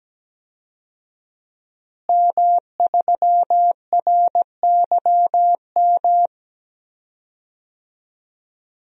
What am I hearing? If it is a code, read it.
Morse code, heard as a keyed tone, about 17 wpm, 700 Hz. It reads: M3RYM
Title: DXCC-100 Call Signs - 5 Characters - Repeated 2x - 17wpm